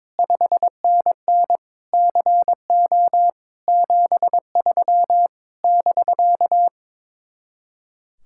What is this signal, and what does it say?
Morse code, heard as a keyed tone, about 22 wpm, 700 Hz. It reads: 5NN CO 73 <BK>
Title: Speed-Racing Common QSO Phrases - 22wpm